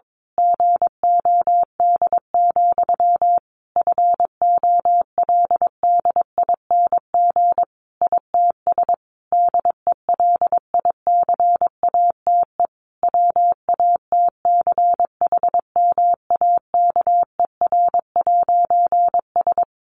Morse code, heard as a keyed tone, about 22 wpm, 700 Hz. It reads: GOD, FOLDING ITS DELICATE WATCHMAKER'S
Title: A Sound Of Thunder 22wpm